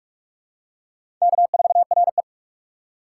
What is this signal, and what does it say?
Morse code, heard as a keyed tone, about 41 wpm, 700 Hz. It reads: K4RE
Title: USA Call Signs - 4 Characters - 6x Character Spacing - 45wpm